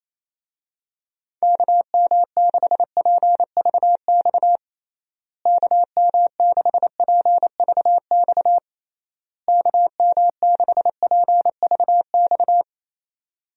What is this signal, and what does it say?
Morse code, heard as a keyed tone, about 28 wpm, 700 Hz. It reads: KM6PVX KM6PVX KM6PVX
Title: USA Call Signs - 6 Characters - Repeated 3x - 28wpm